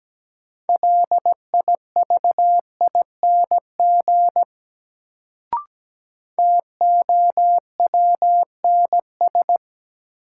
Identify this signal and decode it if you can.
Morse code, heard as a keyed tone, about 17 wpm, 700 Hz. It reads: LIVING TOWNS
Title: Sentences from Top 1000 Words - Review Part 6 - New Word Drill - 17wpm